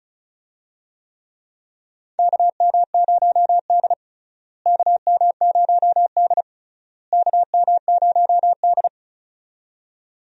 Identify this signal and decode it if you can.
Morse code, heard as a keyed tone, about 36 wpm, 700 Hz. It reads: KM0D KM0D KM0D
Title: USA Call Signs - 4 Characters - Repeated 3x - 35wpm